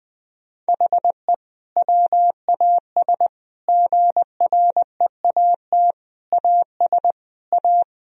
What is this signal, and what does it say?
Morse code, heard as a keyed tone, about 20 wpm, 700 Hz. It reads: HE WAS GREAT AS A
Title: Sentences from Top 1000 Words - Review Part 3 - 20wpm